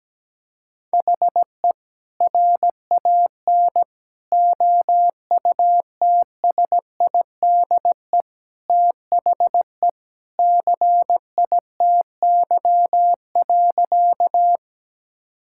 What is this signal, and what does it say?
Morse code, heard as a keyed tone, about 17 wpm, 700 Hz. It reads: HE RAN OUTSIDE THE CITY.
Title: Sentences from Top 1000 Words - Review Part 6 - 17wpm